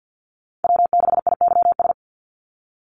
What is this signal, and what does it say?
Morse code, heard as a keyed tone, about 50 wpm, 700 Hz. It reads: W6IYS
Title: Call Signs - 50wpm